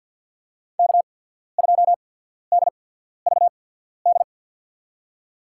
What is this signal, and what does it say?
Morse code, heard as a keyed tone, about 54 wpm, spaced as for 14 wpm, 700 Hz. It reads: K2DVD
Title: USA Call Signs - 5 Characters - 8x Character Spacing - 50wpm